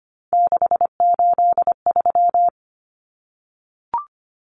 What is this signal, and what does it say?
Morse code, heard as a keyed tone, about 25 wpm, 700 Hz. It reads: 683
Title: Numbers - 3-Digit 25wpm